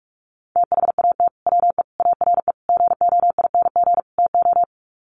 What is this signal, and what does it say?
Morse code, heard as a keyed tone, about 44 wpm, 700 Hz. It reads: THAT WE ARE GOING TO
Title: Sets of 5 Words Encore 45wpm